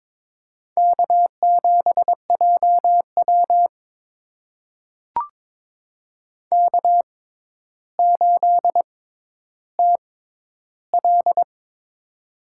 Morse code, heard as a keyed tone, about 23 wpm, spaced as for 11 wpm, 700 Hz. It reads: K7JW K8TL
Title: USA Call Signs - 4 Characters - 6x Character Spacing - 22wpm